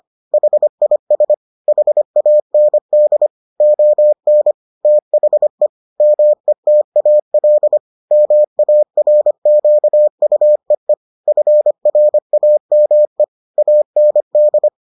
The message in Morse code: HIS HAND ON THE METAL MARQUEE FRAME AND

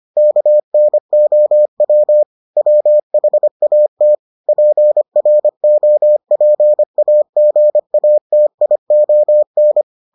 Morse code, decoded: KNOW WHAT PROPAGATION

